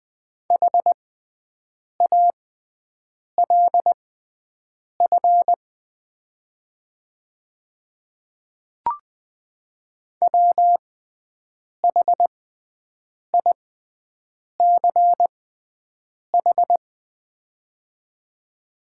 Morse code, heard as a keyed tone, about 20 wpm, spaced as for 7 wpm, 700 Hz. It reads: HALF WHICH
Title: 6x Character Spacing - Top 500 Words - 20wpm